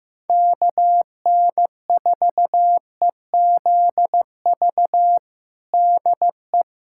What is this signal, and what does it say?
Morse code, heard as a keyed tone, about 16 wpm, 700 Hz. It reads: KN4EZV DE
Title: USA Callsign DE Callsign 15wpm